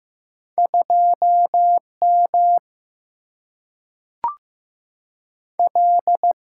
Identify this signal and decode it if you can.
Morse code, heard as a keyed tone, about 15 wpm, 700 Hz. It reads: 2 M L
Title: Single-Letter Single-Number 15wpm